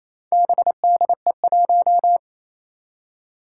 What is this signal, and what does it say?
Morse code, heard as a keyed tone, about 28 wpm, 700 Hz. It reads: BDE1
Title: Four Challenging Letter-Number Combinations 28wpm